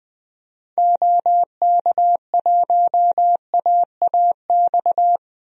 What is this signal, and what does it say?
Morse code, heard as a keyed tone, about 20 wpm, 700 Hz. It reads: OK1AAX
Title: DXCC-100 Call Signs - 6 Characters - Repeated 2x - 20wpm